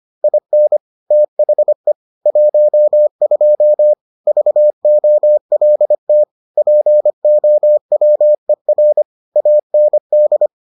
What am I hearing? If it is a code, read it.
Morse code, heard as a keyed tone, about 25 wpm, 600 Hz. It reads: IN THE 12 VOLT POWER AND